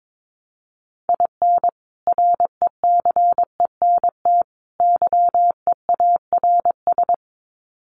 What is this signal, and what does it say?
Morse code, heard as a keyed tone, about 22 wpm, 700 Hz. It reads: IN RECENT YEARS
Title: Sets of 3 Words 22wpm